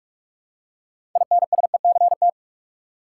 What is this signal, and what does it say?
Morse code, heard as a keyed tone, about 46 wpm, 700 Hz. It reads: INSECT
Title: Six Letter Words - 45wpm